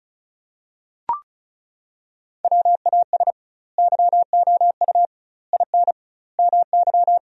Morse code, heard as a keyed tone, about 36 wpm, 700 Hz. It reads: WAS YOU IN MY
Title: 2x Word-Spacing - Sentences from Top 300 Words - 35wpm